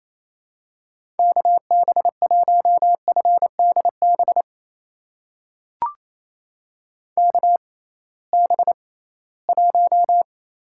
Morse code, heard as a keyed tone, about 28 wpm, spaced as for 18 wpm, 700 Hz. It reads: KB1FDB KB1
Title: USA Call Signs - 6 Characters - 6x Character Spacing - 28wpm